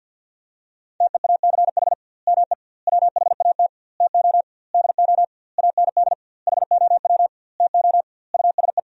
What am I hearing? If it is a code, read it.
Morse code, heard as a keyed tone, about 49 wpm, 700 Hz. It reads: TEACH ME WHAT TO DO AND HOW TO USE
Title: 2x Word-Spacing - Sentences from Top 500 Words - 50wpm